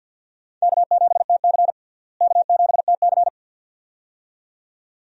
Morse code, heard as a keyed tone, about 50 wpm, 700 Hz. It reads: K7TC K7TC
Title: USA Call Signs - 4 Characters - Repeated 2x - 50wpm